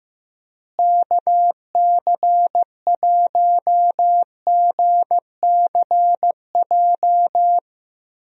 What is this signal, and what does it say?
Morse code, heard as a keyed tone, about 15 wpm, 700 Hz. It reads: KC1GCJ
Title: USA Call Signs - 6 Characters - Repeated 3x - 15wpm